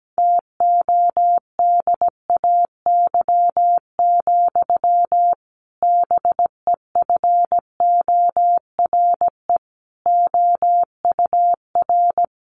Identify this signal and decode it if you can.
Morse code, heard as a keyed tone, about 17 wpm, 700 Hz. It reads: TODAY, BEFORE OUR
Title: A Sound Of Thunder 17wpm